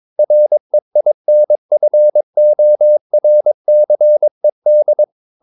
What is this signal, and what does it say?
Morse code, heard as a keyed tone, about 22 wpm, 600 Hz. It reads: REINFORCED
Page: Romanen Never af Ken Follet på engelsk.